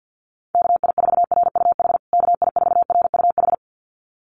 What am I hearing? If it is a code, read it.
Morse code, heard as a keyed tone, about 50 wpm, 700 Hz. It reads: KI4RUH KI4RUH
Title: USA Call Signs Repeated 2x 50wpm